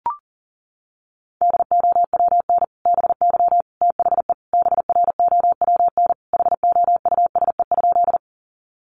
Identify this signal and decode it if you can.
Morse code, heard as a keyed tone, about 40 wpm, 700 Hz. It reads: DOWN BY THE BROWN HOUSE?
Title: Sentences from Top 1200 Words 40wpm